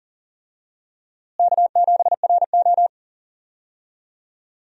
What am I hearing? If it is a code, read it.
Morse code, heard as a keyed tone, about 38 wpm, 700 Hz. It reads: K7RO
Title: USA Call Signs - 4 Characters - Repeated 2x - 40wpm